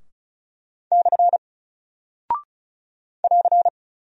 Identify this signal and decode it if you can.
Morse code, heard as a keyed tone, about 36 wpm, 700 Hz. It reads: / <AR>
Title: Single Character (Letter:Number:Punctuation:Prosign) Alt 35wpm